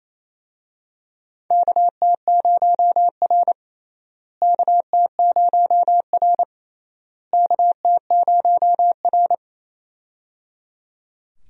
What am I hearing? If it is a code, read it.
Morse code, heard as a keyed tone, about 28 wpm, 700 Hz. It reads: KT0R KT0R KT0R
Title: USA Call Signs - 4 Characters - Repeated 3x - 28wpm